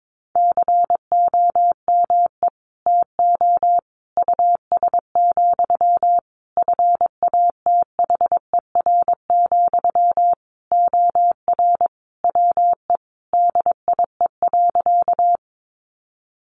Morse code, heard as a keyed tone, about 22 wpm, 700 Hz. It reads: COME TO US, FATHER, OR WE DIE.
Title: Sentences from Top 1300 Words 22wpm